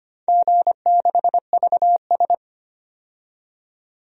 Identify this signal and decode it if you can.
Morse code, heard as a keyed tone, about 25 wpm, 700 Hz. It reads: G6VS
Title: DXCC-100 Call Signs - 4 Characters - 25wpm